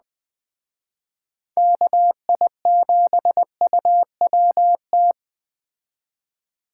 Morse code, heard as a keyed tone, about 20 wpm, 700 Hz. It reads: KI7UWT
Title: Speed Racing - USA 2x3 Call Signs 17wpm